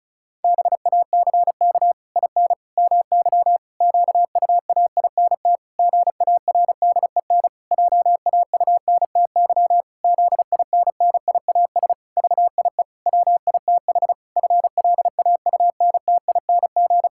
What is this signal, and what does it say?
Morse code, heard as a keyed tone, about 34 wpm, 700 Hz. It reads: BACK IN MY QUAINT GARDEN JAUNTY ZINNIAS VIE WITH FLAUNTING
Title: Pangram Challenge v2 - 35wpm